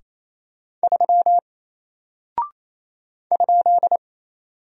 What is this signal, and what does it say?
Morse code, heard as a keyed tone, about 27 wpm, 700 Hz. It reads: 3 ?
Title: Single Character (Letter:Number:Punctuation:Prosign) 28wpm